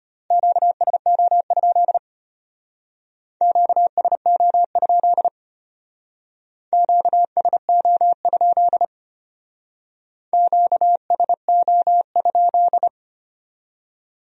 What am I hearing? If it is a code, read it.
Morse code, heard as a keyed tone, about 33 wpm, 700 Hz. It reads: QSO? QSO? QSO? QSO?